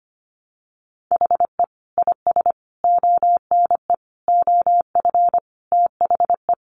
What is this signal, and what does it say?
Morse code, heard as a keyed tone, about 25 wpm, 690 Hz. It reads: HE IS ONE OF THE
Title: Sets of 5 Words Encore 25wpm